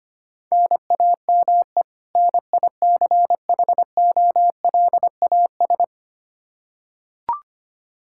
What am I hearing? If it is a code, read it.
Morse code, heard as a keyed tone, about 25 wpm, 700 Hz. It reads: NAME NICHOLAS
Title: K1USN SST QSO Practice 25wpm